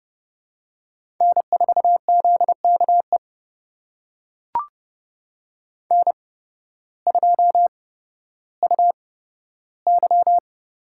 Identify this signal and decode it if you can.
Morse code, heard as a keyed tone, about 30 wpm, spaced as for 14 wpm, 700 Hz. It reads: N4ZKE N2UY
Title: USA Call Signs - 5 Characters - 8x Character Spacing - 30wpm